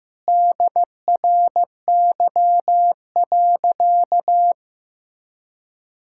Morse code, heard as a keyed tone, about 15 wpm, 700 Hz. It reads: DRY.
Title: Sentences from Top 1600 Words 15wpm